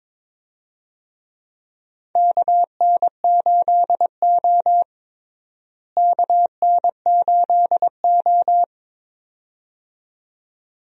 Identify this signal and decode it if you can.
Morse code, heard as a keyed tone, about 22 wpm, 700 Hz. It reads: KN8O KN8O
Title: USA Call Signs - 4 Characters - Repeated 2x - 22wpm